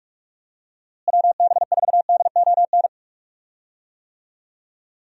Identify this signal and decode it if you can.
Morse code, heard as a keyed tone, about 46 wpm, 700 Hz. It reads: WB4DON
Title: USA Call Signs - 6 Characters - Repeated 2x - 45wpm